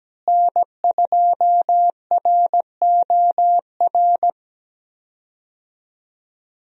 Morse code, heard as a keyed tone, about 17 wpm, 700 Hz. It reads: N2ROR
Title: USA Call Signs - 5 Characters - Repeated 2x - 17wpm